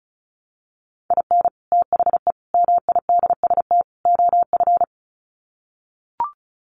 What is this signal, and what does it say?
Morse code, heard as a keyed tone, about 34 wpm, 700 Hz. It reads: IN THE MIDST OF
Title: Sets of 4 Words 35wpm